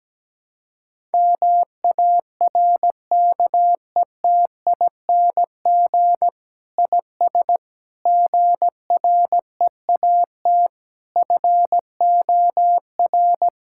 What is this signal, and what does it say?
Morse code, heard as a keyed tone, about 17 wpm, 700 Hz. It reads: MARKETING IS GREAT FOR
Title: Sentences from Top 1000 Words - Review Part 5 - 17wpm